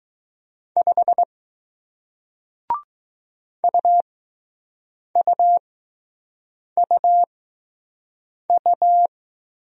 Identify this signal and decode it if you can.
Morse code, heard as a keyed tone, about 21 wpm, spaced as for 6 wpm, 700 Hz. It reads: 5 UUUU